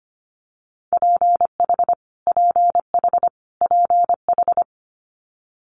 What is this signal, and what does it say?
Morse code, heard as a keyed tone, about 25 wpm, 700 Hz. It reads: PH PH PH
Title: Multi-letter Phonemes Repeated 3x 25wpm